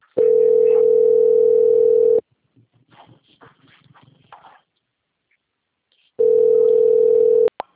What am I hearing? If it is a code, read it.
Ringback tone (North America), 440 + 480 Hz.